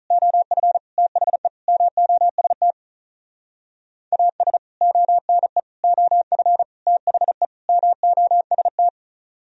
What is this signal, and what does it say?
Morse code, heard as a keyed tone, about 44 wpm, 700 Hz. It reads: OF THE MOST AS ONE OF THE MOST